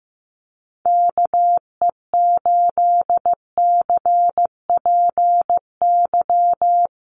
Morse code, heard as a keyed tone, about 15 wpm, 700 Hz. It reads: KE8CPY
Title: USA Call Signs Repeated 3x 15wpm